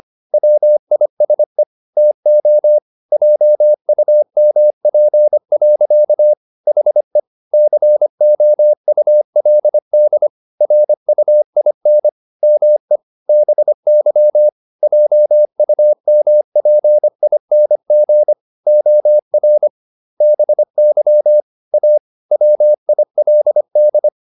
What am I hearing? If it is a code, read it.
Morse code, heard as a keyed tone, about 25 wpm, 600 Hz. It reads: WISE TO JUMP. HE COULD RUIN ME BY JUMPING OR BY A WILD
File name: the old man and the sea part 1 25wpm